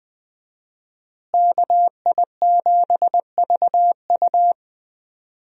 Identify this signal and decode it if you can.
Morse code, heard as a keyed tone, about 20 wpm, 700 Hz. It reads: KI7VU
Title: USA Call Signs - 5 Characters - Repeated 2x - 20wpm